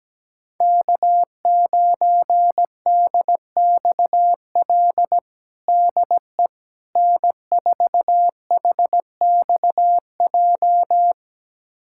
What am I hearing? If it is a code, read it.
Morse code, heard as a keyed tone, about 17 wpm, 700 Hz. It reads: K9DXL DE N4HXJ
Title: USA Callsign DE Callsign - 5 Characters - 17wpm